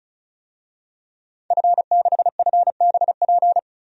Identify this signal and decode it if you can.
Morse code, heard as a keyed tone, about 34 wpm, 700 Hz. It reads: F6FBP
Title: DXCC-100 Call Signs - 5 Characters - 35wpm